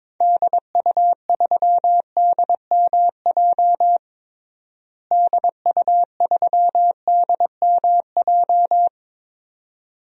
Morse code, heard as a keyed tone, about 22 wpm, 700 Hz. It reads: DU3DMJ DU3DMJ
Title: DXCC-100 Call Signs - 6 Characters - Repeated 2x - 22wpm